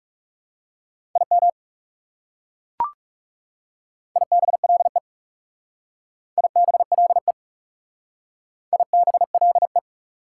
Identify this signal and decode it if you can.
Morse code, heard as a keyed tone, about 36 wpm, 700 Hz. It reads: IM IBLE IBLE IBLE
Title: Speed-Racing Prefixes and Suffixes 30wpm